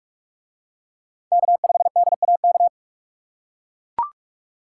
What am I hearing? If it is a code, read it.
Morse code, heard as a keyed tone, about 43 wpm, 700 Hz. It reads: K5DAK